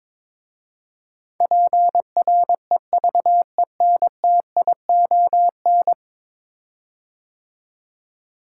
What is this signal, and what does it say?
Morse code, heard as a keyed tone, about 22 wpm, 700 Hz. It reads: PREVENTION
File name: practice / 22wpm / 10 Letter Words - First 3 Letters Given - 22wpm